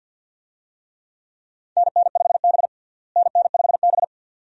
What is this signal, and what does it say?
Morse code, heard as a keyed tone, about 52 wpm, 700 Hz. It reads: NN5B NN5B
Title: USA Call Signs - 4 Characters - Repeated 2x - 50wpm